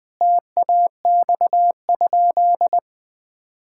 Morse code, heard as a keyed tone, about 20 wpm, 700 Hz. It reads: TAX?
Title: Sentences from Top 1000 Words - Review Part 5 - 20wpm